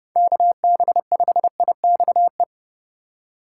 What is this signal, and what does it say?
Morse code, heard as a keyed tone, about 30 wpm, 700 Hz. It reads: KB5IXE
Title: USA Call Signs - 6 Characters - 4x Character Spacing - 30wpm